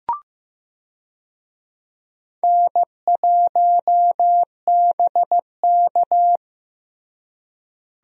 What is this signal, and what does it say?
Morse code, heard as a keyed tone, about 15 wpm, 700 Hz. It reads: N1BK
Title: USA Call Signs - 4 Characters - Repeated 2x - 15wpm